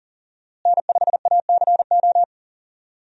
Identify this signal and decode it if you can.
Morse code, heard as a keyed tone, about 38 wpm, 700 Hz. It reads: N5ACO
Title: USA Call Signs - 5 Characters - Repeated 3x - 40wpm